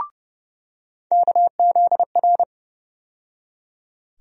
Morse code, heard as a keyed tone, about 30 wpm, 700 Hz. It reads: KZR
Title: USA Call Sign Postfixes - 3 Letters 30wpm